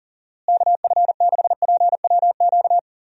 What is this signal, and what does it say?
Morse code, heard as a keyed tone, about 40 wpm, 700 Hz. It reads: KF6PWQ
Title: USA Call Signs - 6 Characters - Repeated 3x - 40wpm